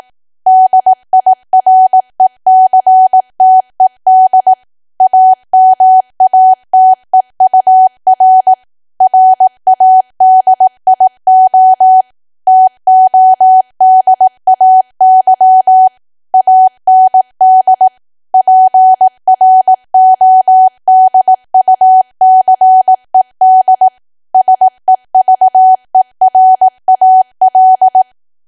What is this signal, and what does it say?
Morse code, heard as a keyed tone, about 18 wpm, 750 Hz. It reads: DIRECTED AMATEUR RADIO TODAY AND PRODUCED SEVERAL
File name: Archive / 18 WPM / 18WPM